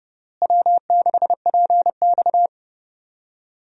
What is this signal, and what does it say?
Morse code, heard as a keyed tone, about 30 wpm, 700 Hz. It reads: W6PX
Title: USA Call Signs - 4 Characters - Repeated 3x - 30wpm